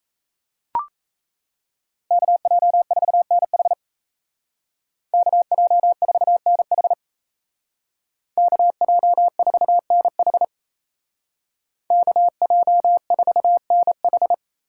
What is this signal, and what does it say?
Morse code, heard as a keyed tone, about 37 wpm, 700 Hz. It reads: KJ4NH KJ4NH KJ4NH KJ4NH